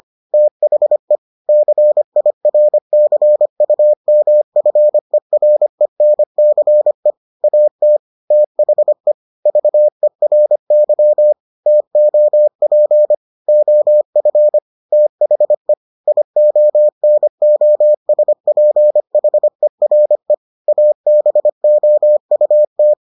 Morse code, heard as a keyed tone, about 25 wpm, 600 Hz. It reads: THE CIRCUMFERENCE AT THE VERY TOP OF THE IONOSPHERE ABOUT